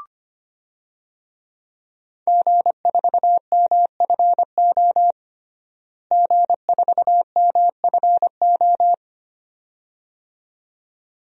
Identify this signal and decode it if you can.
Morse code, heard as a keyed tone, about 25 wpm, 700 Hz. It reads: G4MFO G4MFO
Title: DXCC-100 Call Signs - 5 Characters - Repeated 2x - 25wpm